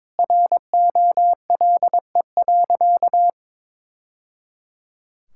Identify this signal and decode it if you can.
Morse code, heard as a keyed tone, about 22 wpm, 700 Hz. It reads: ROLE.
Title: Sentences from Top 1000 Words - Review Part 6 - 22wpm